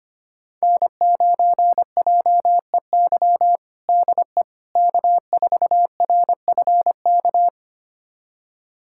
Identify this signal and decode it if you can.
Morse code, heard as a keyed tone, about 25 wpm, 700 Hz. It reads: N9JEY DE K4RFK
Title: USA Callsign DE Callsign - 5 Characters - 25wpm